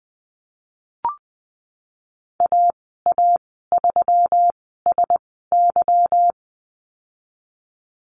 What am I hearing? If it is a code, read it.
Morse code, heard as a keyed tone, about 20 wpm, spaced as for 14 wpm, 700 Hz. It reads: AA3SY
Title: 173 Lesson 28g - Number 9 - Call Signs 20wpm